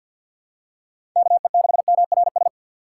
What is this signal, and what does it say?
Morse code, heard as a keyed tone, about 51 wpm, 700 Hz. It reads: KE6MRS
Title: USA Call Signs - 6 Characters - 4x Character Spacing - 50wpm